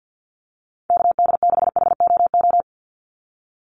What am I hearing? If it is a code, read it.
Morse code, heard as a keyed tone, about 49 wpm, 700 Hz. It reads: KD6HOO
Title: USA Call Signs 50wpm